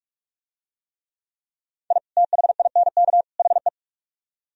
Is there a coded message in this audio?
Yes: Morse code from 1.9 to 3.7 s, heard as a keyed tone, about 44 wpm, 700 Hz. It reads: I THINK HE